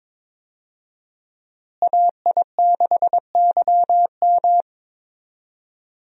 Morse code, heard as a keyed tone, about 22 wpm, 700 Hz. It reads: AI6YM
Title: USA Call Signs - 5 Characters - Repeated 3x - 22wpm